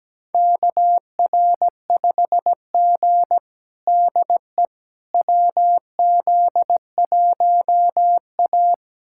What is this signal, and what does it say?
Morse code, heard as a keyed tone, about 17 wpm, 700 Hz. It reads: KR5G DE WZ1A
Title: USA Callsign DE Callsign - 4 Characters - 17wpm